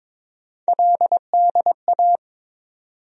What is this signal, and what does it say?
Morse code, heard as a keyed tone, about 22 wpm, 700 Hz. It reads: LDA
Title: USA Call Sign Postfixes - 3 Letters 22wpm